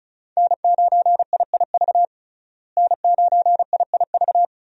Morse code, heard as a keyed tone, about 34 wpm, 700 Hz. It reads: N9IIV N9IIV
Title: USA Call Signs - 5 Characters - Repeated 2x - 35wpm